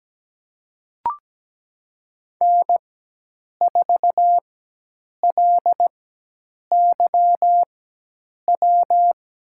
Morse code, heard as a keyed tone, about 17 wpm, spaced as for 8 wpm, 700 Hz. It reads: N4LYW
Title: USA Call Signs - 5 Characters - 4x Character Spacing - 17wpm